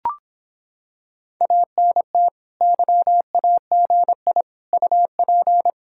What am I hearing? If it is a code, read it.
Morse code, heard as a keyed tone, about 26 wpm, 700 Hz. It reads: ANT YAGI UP